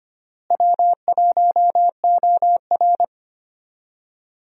Morse code, heard as a keyed tone, about 25 wpm, 700 Hz. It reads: W1OR
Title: USA Call Signs - 4 Characters - Repeated 3x - 25wpm